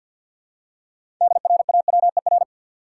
Morse code, heard as a keyed tone, about 52 wpm, 700 Hz. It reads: DRAWER
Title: Six Letter Words - 50wpm